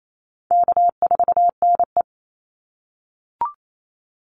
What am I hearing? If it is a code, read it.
Morse code, heard as a keyed tone, about 28 wpm, 700 Hz. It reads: K4NE
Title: Easy Call Signs - 28wpm